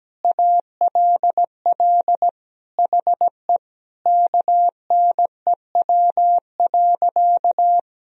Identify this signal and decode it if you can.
Morse code, heard as a keyed tone, about 17 wpm, 700 Hz. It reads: ALL HE KNEW.
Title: Sentences from Top 400 Words and Abbreviations 17wpm